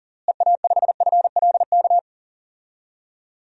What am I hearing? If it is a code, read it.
Morse code, heard as a keyed tone, about 37 wpm, 700 Hz. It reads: EA5FLK
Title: DXCC-100 Call Signs - 6 Characters - Repeated 3x - 40wpm